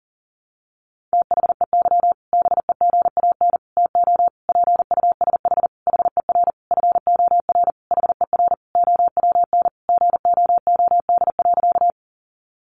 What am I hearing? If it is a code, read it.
Morse code, heard as a keyed tone, about 40 wpm, 700 Hz. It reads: THEY BEGAN TO PUSH HER FOR HER OWN GOOD.
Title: Sentences from Top 1300 Words 40wpm